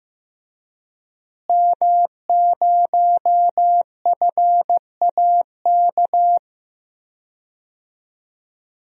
Morse code, heard as a keyed tone, about 15 wpm, 700 Hz. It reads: M0FAK
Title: DXCC-100 Call Signs - 5 Characters - Repeated 3x - 15wpm